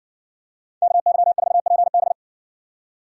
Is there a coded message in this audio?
Yes: Morse code from 0.8 to 2.1 s, heard as a keyed tone, about 55 wpm, 700 Hz. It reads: KY4PB